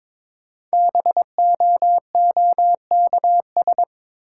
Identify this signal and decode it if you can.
Morse code, heard as a keyed tone, about 22 wpm, 700 Hz. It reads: BOOKS
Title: Sentences from Top 1000 Words - Review Part 4 - New Word Drill - 22wpm